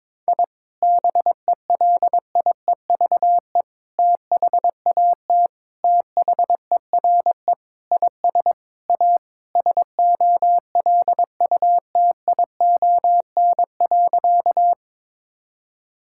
Morse code, heard as a keyed tone, about 22 wpm, 700 Hz. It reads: I BELIEVE THAT THERE IS A SOLUTION.
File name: practice / 22wpm / Sentences with 4-Word Phrases - 22wpm